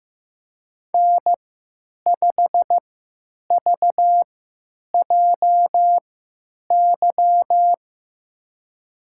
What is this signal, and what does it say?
Morse code, heard as a keyed tone, about 15 wpm, spaced as for 9 wpm, 700 Hz. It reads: N5VJY
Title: USA Call Signs - 3x Character Spacing - 15wpm